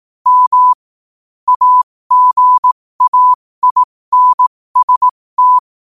Morse code, heard as a keyed tone, about 18 wpm, spaced as for 15 wpm, 1000 Hz. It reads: M AGAINST
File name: qotdfiles / 15wpm